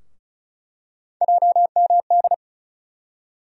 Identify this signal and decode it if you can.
Morse code, heard as a keyed tone, about 36 wpm, 700 Hz. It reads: JMD
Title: USA Call Sign Postfixes - 3 Letters 35wpm